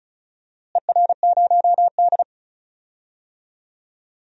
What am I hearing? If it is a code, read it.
Morse code, heard as a keyed tone, about 34 wpm, 700 Hz. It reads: ER0D
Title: DXCC-100 Call Signs - 4 Characters - Repeated 3x - 35wpm